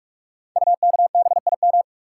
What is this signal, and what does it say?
Morse code, heard as a keyed tone, about 44 wpm, 700 Hz. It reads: UKBIM
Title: Code Groups - 5 Letters - 45wpm